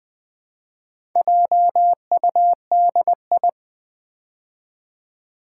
Morse code, heard as a keyed tone, about 20 wpm, 700 Hz. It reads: JUDI
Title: Build A Word - 2 Letters At A Time - 8-Letter Words - 20wpm